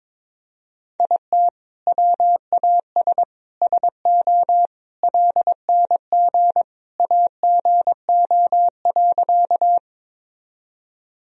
Morse code, heard as a keyed tone, about 22 wpm, 700 Hz. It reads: IT WAS SO LNG AGO.
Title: Sentences from Top 500 Words and Abbreviations 22wpm